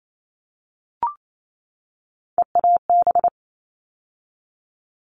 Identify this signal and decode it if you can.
Morse code, heard as a keyed tone, about 28 wpm, 700 Hz. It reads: EAB
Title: All Cut Numbers - 3-Digit 28wpm